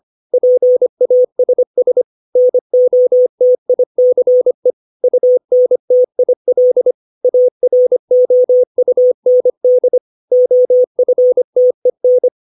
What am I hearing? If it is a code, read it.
Morse code, heard as a keyed tone, about 25 wpm, 500 Hz. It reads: PASS NOTICE UNTIL AROUND OFTEN
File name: 632 words / 632 Words 25WPM